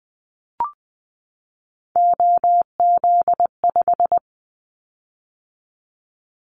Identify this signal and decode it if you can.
Morse code, heard as a keyed tone, about 20 wpm, 700 Hz. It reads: OZ5
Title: Callsign Prefixes for Top 100 DXCC - 20wpm